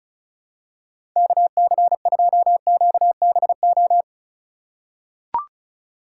Morse code, heard as a keyed tone, about 36 wpm, 700 Hz. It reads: KC2QBO
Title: USA Call Signs - 6 Characters - 2x Character Spacing - 35wpm